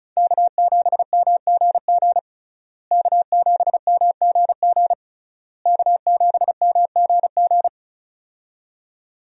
Morse code, heard as a keyed tone, about 36 wpm, 700 Hz. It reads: K7MGG K7MGG K7MGG
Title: USA Call Signs - 5 Characters - Repeated 3x - 35wpm